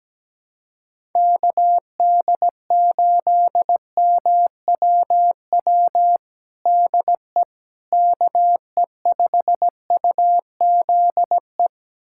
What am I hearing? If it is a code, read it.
Morse code, heard as a keyed tone, about 17 wpm, 700 Hz. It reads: KD8MWW DE KE5UZE
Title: USA Callsign DE Callsign - 6 Characters - 17wpm